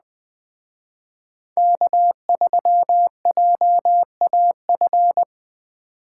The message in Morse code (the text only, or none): K3JAF